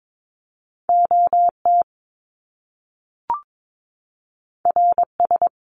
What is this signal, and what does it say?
Morse code, heard as a keyed tone, about 22 wpm, 700 Hz. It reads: OT RS
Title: Top Two Letter Combinations in Words 22wpm